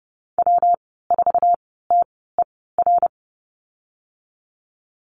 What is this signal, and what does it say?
Morse code, heard as a keyed tone, about 30 wpm, spaced as for 17 wpm, 700 Hz. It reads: W4TER
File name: Lesson 19g - Letter w - Call Signs 30wpm